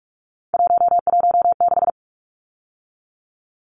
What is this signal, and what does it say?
Morse code, heard as a keyed tone, about 44 wpm, 700 Hz. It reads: 116
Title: Numbers - 3-Digit 45wpm